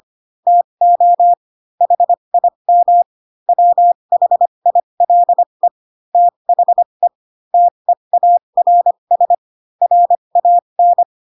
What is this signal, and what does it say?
Morse code, heard as a keyed tone, about 25 wpm, spaced as for 22 wpm, 700 Hz. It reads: TO HIM WHILE THE TEARS RAN
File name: Pinocchio / Pinocchio ch7 18 25 ik1hge